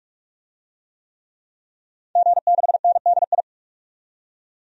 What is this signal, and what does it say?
Morse code, heard as a keyed tone, about 46 wpm, 700 Hz. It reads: G6NDI
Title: DXCC-100 Call Signs - 5 Characters - 45wpm